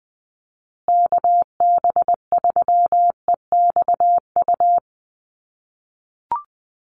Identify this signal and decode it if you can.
Morse code, heard as a keyed tone, about 20 wpm, 700 Hz. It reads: KB3EXU
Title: USA Call Signs Repeated 2x 20wpm